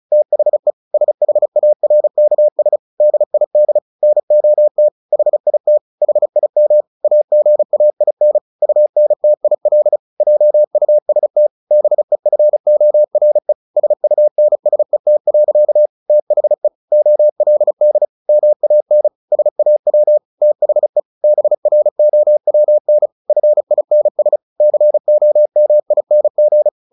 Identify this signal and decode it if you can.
Morse code, heard as a keyed tone, about 34 wpm, 600 Hz. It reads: THE SHARKS DID NOT HIT HIM AGAIN UNTIL JUST BEFORE SUNSET. THE OLD MAN SAW THE BROWN FINS COMING
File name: the old man and the sea 2 35wpm